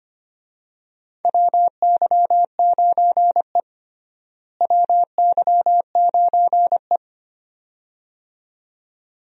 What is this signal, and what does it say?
Morse code, heard as a keyed tone, about 25 wpm, 700 Hz. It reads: WY9E WY9E
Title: USA Call Signs - 4 Characters - Repeated 2x - 25wpm